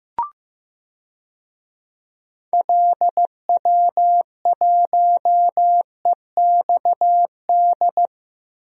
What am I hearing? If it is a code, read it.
Morse code, heard as a keyed tone, about 16 wpm, 700 Hz. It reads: LW1EXD
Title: DXCC-100 Call Signs - 6 Characters - Repeated 2x - 15wpm